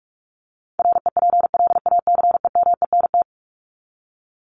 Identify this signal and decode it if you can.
Morse code, heard as a keyed tone, about 44 wpm, 700 Hz. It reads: REPLACEMENT
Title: Top 1751-2000 Words 45wpm